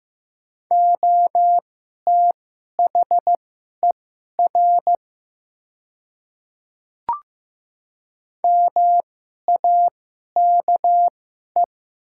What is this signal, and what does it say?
Morse code, heard as a keyed tone, about 15 wpm, spaced as for 11 wpm, 690 Hz. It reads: OTHER MAKE
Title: 2x Character Spacing - Top 100 Words - 15wpm